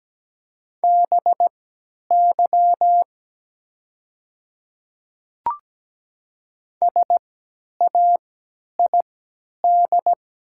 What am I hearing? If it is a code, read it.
Morse code, heard as a keyed tone, about 17 wpm, spaced as for 10 wpm, 700 Hz. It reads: BY SAID
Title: 3x Character Spacing - Top 100 Words - 17wpm